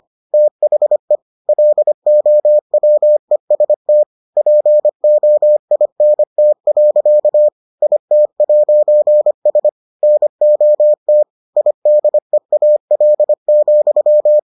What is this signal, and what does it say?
Morse code, heard as a keyed tone, about 25 wpm, 600 Hz. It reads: THE LOWEST POINT. IT'S NOT IDEAL,